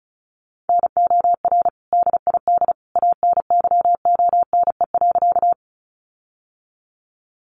Morse code, heard as a keyed tone, about 34 wpm, 700 Hz. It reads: NOR DID ANYONE.
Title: Sentences from Top 1500 Words 35wpm